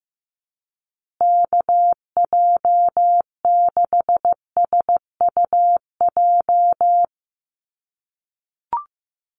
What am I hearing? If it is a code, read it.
Morse code, heard as a keyed tone, about 15 wpm, 700 Hz. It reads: KJ6SUJ
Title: USA Call Signs Repeated 2x 15wpm